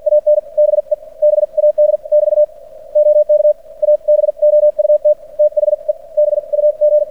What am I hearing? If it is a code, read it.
Morse code, heard as a keyed tone, about 53 wpm, spaced as for 44 wpm, 600 Hz. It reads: AN BE BAD <BT> OK ABOUT THE BUG
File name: archived / CRSnet